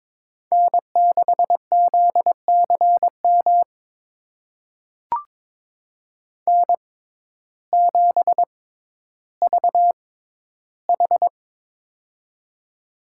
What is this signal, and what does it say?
Morse code, heard as a keyed tone, about 22 wpm, spaced as for 12 wpm, 700 Hz. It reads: N6ZCM N7VH
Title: USA Call Signs - 6x Character Spacing - 22wpm